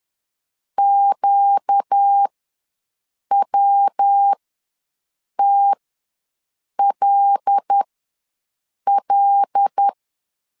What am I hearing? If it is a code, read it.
Morse code, heard as a keyed tone, about 10 wpm, spaced as for 6 wpm, 800 Hz. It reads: QWTLL
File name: LEZIONI / 5 LBVC / 5 L ws 10wpm 3636